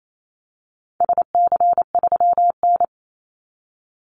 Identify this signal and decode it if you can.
Morse code, heard as a keyed tone, about 28 wpm, 700 Hz. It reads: SC3N
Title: Call Signs - 28wpm